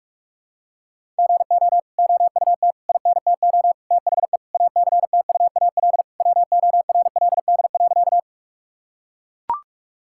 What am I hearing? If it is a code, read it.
Morse code, heard as a keyed tone, about 41 wpm, 700 Hz. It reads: GO OUT INTO THE ACTUAL WORLD.
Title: Sentences from Top 1000 Words - Review Part 5 - 45wpm